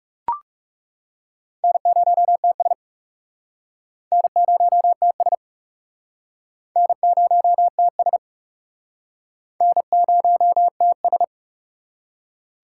Morse code, heard as a keyed tone, about 39 wpm, 700 Hz. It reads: N0TS N0TS N0TS N0TS